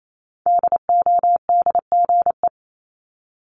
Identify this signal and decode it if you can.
Morse code, heard as a keyed tone, about 28 wpm, 700 Hz. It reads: DODGE
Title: Multi-letter Phonemes Repeated 3x 28wpm